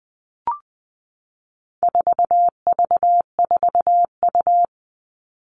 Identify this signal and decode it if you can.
Morse code, heard as a keyed tone, about 20 wpm, 700 Hz. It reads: 4V4U
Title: All Cut Numbers - 4-Digit 20wpm